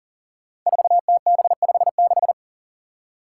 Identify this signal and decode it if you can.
Morse code, heard as a keyed tone, about 40 wpm, 700 Hz. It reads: 4TB56
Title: Five Challenging Letter-Number Combinations 40wpm